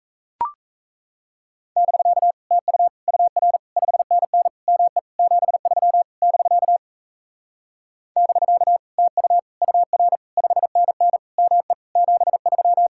Morse code, heard as a keyed tone, about 36 wpm, 700 Hz. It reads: <BK> TU UR 5NN ME 73 <BK> <BK> TU UR 5NN ME 73